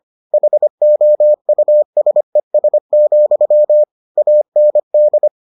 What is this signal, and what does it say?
Morse code, heard as a keyed tone, about 25 wpm, 600 Hz. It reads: HOUSES, AND